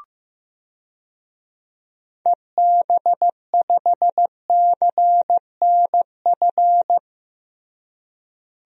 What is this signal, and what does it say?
Morse code, heard as a keyed tone, about 15 wpm, 700 Hz. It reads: EB5CNF
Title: DXCC-100 Call Signs - 6 Characters - Repeated 2x - 15wpm